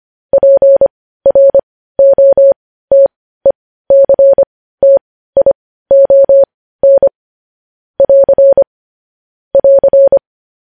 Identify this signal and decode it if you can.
Morse code, heard as a keyed tone, about 25 wpm, spaced as for 15 wpm, 550 Hz. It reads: PROTECTION <AR> <AR>
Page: news-headlines-in-morse-code-at-15-wpm